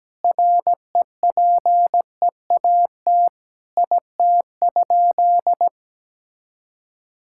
Morse code, heard as a keyed tone, about 17 wpm, 700 Hz. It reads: REPEAT IT?
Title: Sentences from Top 1600 Words 17wpm